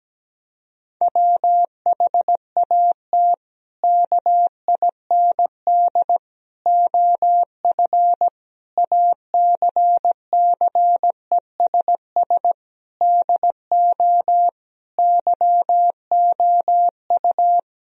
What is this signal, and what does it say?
Morse code, heard as a keyed tone, about 17 wpm, 700 Hz. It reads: WHAT KIND OF ACCESS DO YOU
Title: Sentences from Top 1000 Words - Review Part 4 - 17wpm